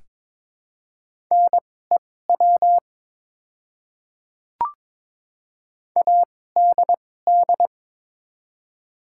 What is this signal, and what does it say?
Morse code, heard as a keyed tone, about 22 wpm, spaced as for 16 wpm, 700 Hz. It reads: NEW ADD
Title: 2x Character Spacing - Top 200 Words - 22wpm